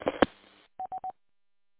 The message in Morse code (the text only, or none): none